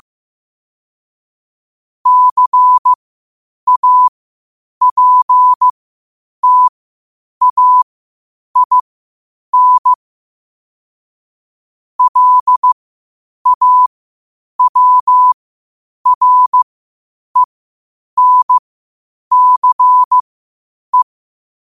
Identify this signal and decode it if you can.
Morse code, heard as a keyed tone, about 15 wpm, spaced as for 8 wpm, 1000 Hz. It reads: CAPTAIN LAWRENCE
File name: qotdfiles / 8wpm